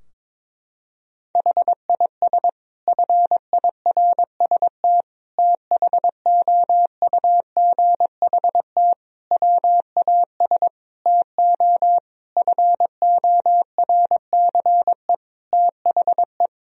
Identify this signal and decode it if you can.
Morse code, heard as a keyed tone, about 22 wpm, 700 Hz. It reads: HIS FIRST THOUGHT WAS TO FORCE THE
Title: Sentences from Top 1000 Words - Review Part 3 - 22wpm